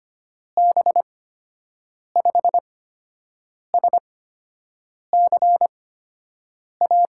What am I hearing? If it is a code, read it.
Morse code, heard as a keyed tone, about 25 wpm, spaced as for 7 wpm, 700 Hz. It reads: B5SCA
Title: USA Call Signs - 6 Characters - 8x Character Spacing - 25wpm